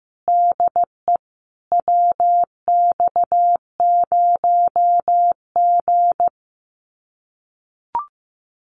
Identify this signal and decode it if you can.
Morse code, heard as a keyed tone, about 15 wpm, 700 Hz. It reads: DE WX0G
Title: Common QSO Phrases Encore 15wpm